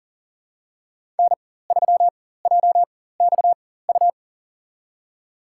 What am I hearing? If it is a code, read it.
Morse code, heard as a keyed tone, about 40 wpm, spaced as for 19 wpm, 700 Hz. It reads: N3JXU
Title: USA Call Signs - 5 Characters - 4x Character Spacing - 40wpm